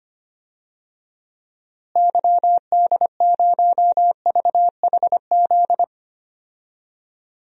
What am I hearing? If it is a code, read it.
Morse code, heard as a keyed tone, about 25 wpm, 700 Hz. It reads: YD0VHZ
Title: DXCC-100 Call Signs - 6 Characters - Repeated 2x - 25wpm